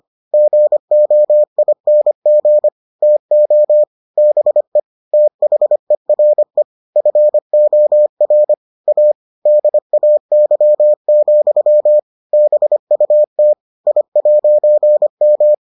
Morse code, heard as a keyed tone, about 25 wpm, 600 Hz. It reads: GOING TO BE THERE FOR A DAY, BUT I'M